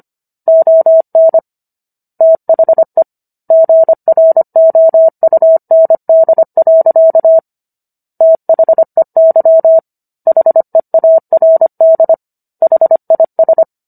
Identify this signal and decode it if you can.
Morse code, heard as a keyed tone, about 25 wpm, 650 Hz. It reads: ON THE GROUND. THEY HEARD HIS